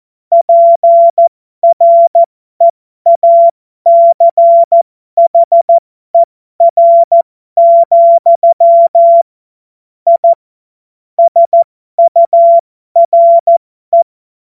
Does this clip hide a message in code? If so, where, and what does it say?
Yes: Morse code from 0.3 to 14.0 s, heard as a keyed tone, about 14 wpm, spaced as for 12 wpm, 680 Hz. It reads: PREACHER, I SURE